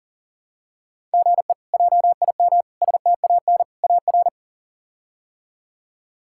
Morse code, heard as a keyed tone, about 40 wpm, 700 Hz. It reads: GE JIM STAN AR
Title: K1USN SST QSO Practice - 40wpm